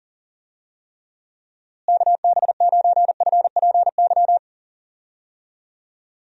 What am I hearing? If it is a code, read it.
Morse code, heard as a keyed tone, about 38 wpm, 700 Hz. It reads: KB9FPY
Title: USA Call Signs - 6 Characters - Repeated 3x - 40wpm